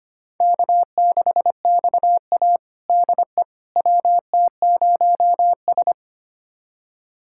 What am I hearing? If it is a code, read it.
Morse code, heard as a keyed tone, about 25 wpm, 700 Hz. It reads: K6XA DE WT0S
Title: USA Callsign DE Callsign - 4 Characters - 25wpm